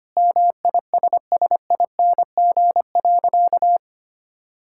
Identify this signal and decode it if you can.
Morse code, heard as a keyed tone, about 25 wpm, 700 Hz. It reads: MISSING.
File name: practice / 25wpm / Sentences with 4-Word Phrases - 25wpm